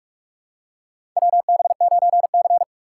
Morse code, heard as a keyed tone, about 47 wpm, 700 Hz. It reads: WB9C
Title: USA Call Signs - 4 Characters - 4x Character Spacing - 45wpm